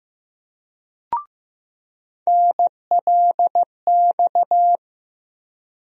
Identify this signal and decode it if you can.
Morse code, heard as a keyed tone, about 15 wpm, 700 Hz. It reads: NLX
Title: USA Call Sign Postfixes - 3 Letters 15wpm